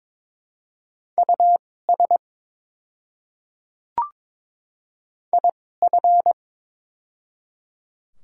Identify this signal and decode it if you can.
Morse code, heard as a keyed tone, about 22 wpm, spaced as for 16 wpm, 690 Hz. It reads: US IF
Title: 2x Character Spacing - Top 500 Words - 22wpm